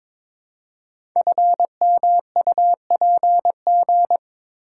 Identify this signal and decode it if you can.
Morse code, heard as a keyed tone, about 22 wpm, 700 Hz. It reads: FMUPG
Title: Code Groups - 5 Letters - 22wpm